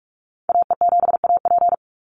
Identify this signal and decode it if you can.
Morse code, heard as a keyed tone, about 46 wpm, 700 Hz. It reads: AE7AP
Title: NA SOTA Activator Callsigns 45wpm